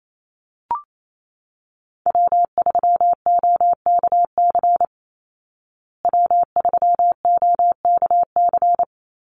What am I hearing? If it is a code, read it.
Morse code, heard as a keyed tone, about 28 wpm, 690 Hz. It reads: W3OKC W3OKC
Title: Call Signs Repeated 3x - 28wpm